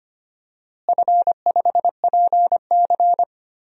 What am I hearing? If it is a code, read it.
Morse code, heard as a keyed tone, about 25 wpm, 700 Hz. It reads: F5PC
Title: DXCC-100 Call Signs - 4 Characters - Repeated 3x - 25wpm